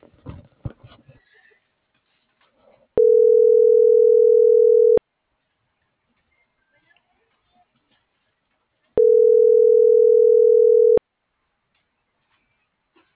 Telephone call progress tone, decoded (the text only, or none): ringback tone